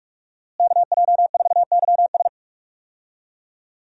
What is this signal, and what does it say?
Morse code, heard as a keyed tone, about 46 wpm, 700 Hz. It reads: KJ4YS
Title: USA Call Signs - 5 Characters - Repeated 3x - 45wpm